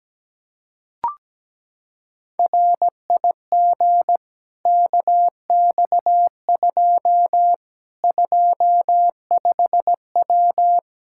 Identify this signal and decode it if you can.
Morse code, heard as a keyed tone, about 17 wpm, 700 Hz. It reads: RIG KX2 25W
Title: K1USN SST QSO Practice 17wpm